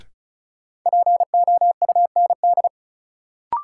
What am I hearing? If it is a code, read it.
Morse code, heard as a keyed tone, about 36 wpm, 700 Hz. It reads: POUND
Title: Sentences from Top 1000 Words - Review Part 2 - New Word Drill - 35wpm